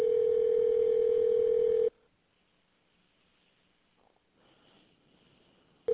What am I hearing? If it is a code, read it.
Ringback tone (North America), 440 + 480 Hz.